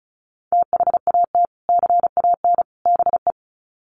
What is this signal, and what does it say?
Morse code, heard as a keyed tone, about 34 wpm, 700 Hz. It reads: THAT CAN BE
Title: Sets of 3 Words 35wpm